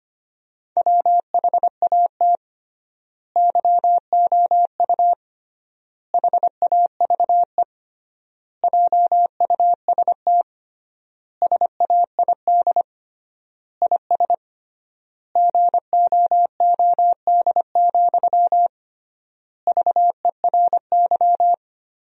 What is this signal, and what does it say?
Morse code, heard as a keyed tone, about 25 wpm, 700 Hz. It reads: WHAT YOU HAVE JUST SAID IS GOOD, VERY
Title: 3x Word-Spacing - Sentences from Top 200 Words - 25wpm